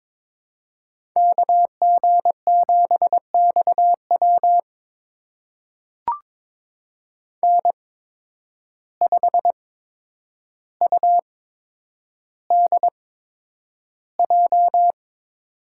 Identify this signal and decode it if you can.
Morse code, heard as a keyed tone, about 22 wpm, spaced as for 9 wpm, 700 Hz. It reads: KG7XW N5UDJ
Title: USA Call Signs - 5 Characters - 8x Character Spacing - 22wpm